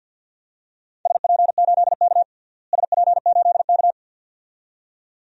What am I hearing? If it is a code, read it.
Morse code, heard as a keyed tone, about 53 wpm, 700 Hz. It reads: SP8K SP8K
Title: DXCC-100 Call Signs - 4 Characters - Repeated 2x - 50wpm